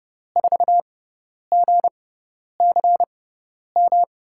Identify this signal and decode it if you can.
Morse code, heard as a keyed tone, about 30 wpm, spaced as for 10 wpm, 700 Hz. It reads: 4GCM
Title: USA Call Signs - 5 Characters - 6x Character Spacing - 30wpm